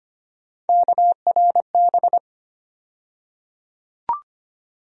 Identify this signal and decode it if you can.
Morse code, heard as a keyed tone, about 25 wpm, 700 Hz. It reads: KRB
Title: USA Call Sign Postfixes - 3 Letters 25wpm